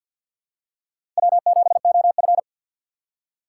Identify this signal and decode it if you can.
Morse code, heard as a keyed tone, about 47 wpm, 700 Hz. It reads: W7OF
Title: USA Call Signs - 4 Characters - 2x Character Spacing - 50wpm